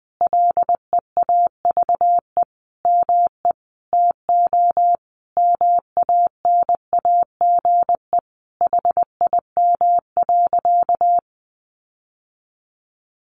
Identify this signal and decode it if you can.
Morse code, heard as a keyed tone, about 20 wpm, 700 Hz. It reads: LEAVE ME TO MANAGE HIM.
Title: Sentences from Top 1400 Words 20wpm